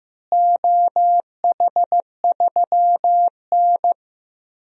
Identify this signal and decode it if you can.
Morse code, heard as a keyed tone, about 15 wpm, 700 Hz. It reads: OH3N
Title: DXCC-100 Call Signs - 4 Characters - Repeated 3x - 15wpm